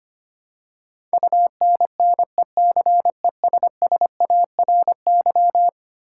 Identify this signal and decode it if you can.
Morse code, heard as a keyed tone, about 25 wpm, 700 Hz. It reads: UNNECESSARY
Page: morse-code-ninja-practice-25wpm